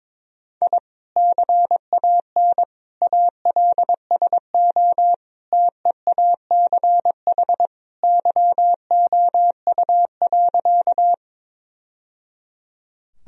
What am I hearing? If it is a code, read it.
Morse code, heard as a keyed tone, about 21 wpm, 700 Hz. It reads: I CAN ALSO TEACH YOU.
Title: Sentences from Top 1000 Words - Review Part 2 - 22wpm